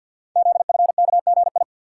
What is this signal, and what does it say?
Morse code, heard as a keyed tone, about 53 wpm, 700 Hz. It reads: ZFKGI
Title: Code Groups - 5 Letters - 50wpm